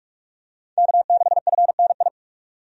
Morse code, heard as a keyed tone, about 46 wpm, 700 Hz. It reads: K6FNI
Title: USA Call Signs - 5 Characters - 45wpm